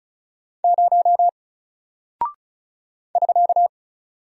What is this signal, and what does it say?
Morse code, heard as a keyed tone, about 36 wpm, 700 Hz. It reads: 0 <SK>
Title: Single Character (Letter:Number:Punctuation:Prosign) Alt 35wpm